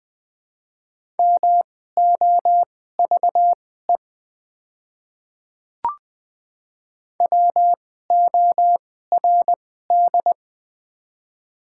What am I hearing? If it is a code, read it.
Morse code, heard as a keyed tone, about 21 wpm, spaced as for 15 wpm, 700 Hz. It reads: MOVE WORD
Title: 2x Character Spacing - Top 300 Words - 20wpm